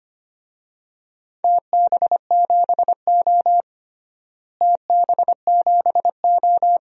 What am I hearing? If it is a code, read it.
Morse code, heard as a keyed tone, about 25 wpm, 700 Hz. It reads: TB7O TB7O
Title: DXCC-100 Call Signs - 4 Characters - Repeated 3x - 25wpm